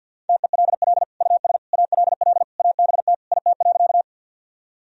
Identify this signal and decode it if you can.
Morse code, heard as a keyed tone, about 51 wpm, 700 Hz. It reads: TELL US ALL ABT IT.
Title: Sentences from Top 200 Words and Abbreviations 50wpm